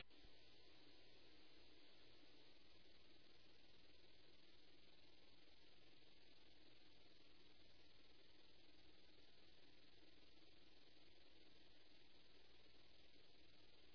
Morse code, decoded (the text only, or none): none